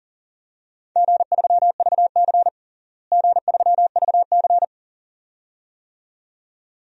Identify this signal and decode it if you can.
Morse code, heard as a keyed tone, about 37 wpm, 700 Hz. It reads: G3VC G3VC
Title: DXCC-100 Call Signs - 4 Characters - Repeated 2x - 40wpm